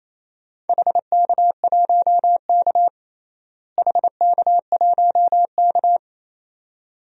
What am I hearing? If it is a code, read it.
Morse code, heard as a keyed tone, about 28 wpm, 700 Hz. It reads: HK1K HK1K
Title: DXCC-100 Call Signs - 4 Characters - Repeated 2x - 28wpm